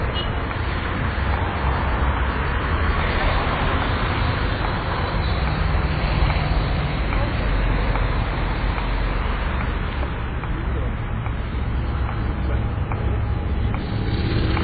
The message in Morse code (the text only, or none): none